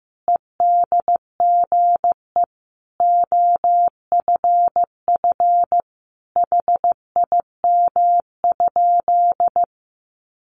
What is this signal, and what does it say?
Morse code, heard as a keyed tone, about 15 wpm, 700 Hz. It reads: EDGE OFF HIM?
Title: Sentences from Top 1300 Words 15wpm